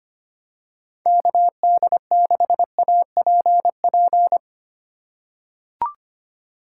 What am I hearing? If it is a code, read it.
Morse code, heard as a keyed tone, about 25 wpm, 700 Hz. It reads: KD6APP
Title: USA Call Signs - 6 Characters - 4x Character Spacing - 25wpm